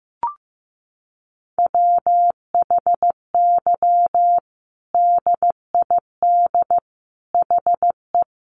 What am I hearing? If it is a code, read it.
Morse code, heard as a keyed tone, about 15 wpm, 700 Hz. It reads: WHY DID HE
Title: Sentences from Top 1500 Words 15wpm